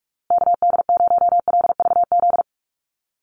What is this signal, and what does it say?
Morse code, heard as a keyed tone, about 44 wpm, 690 Hz. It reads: KD0LVZ
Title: USA Call Signs Repeated 3x 45wpm